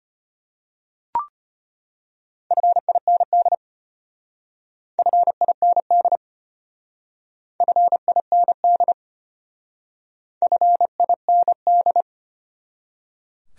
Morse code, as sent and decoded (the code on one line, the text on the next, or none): ..-. .. -. -.. / ..-. .. -. -.. / ..-. .. -. -.. / ..-. .. -. -..
FIND FIND FIND FIND